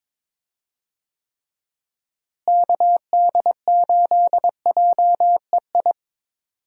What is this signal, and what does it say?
Morse code, heard as a keyed tone, about 22 wpm, 700 Hz. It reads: KD8JEI
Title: USA Call Signs - 6 Characters - Repeated 3x - 22wpm